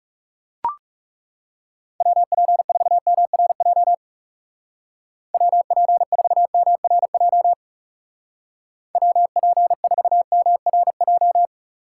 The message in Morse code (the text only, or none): WP4MRJ WP4MRJ WP4MRJ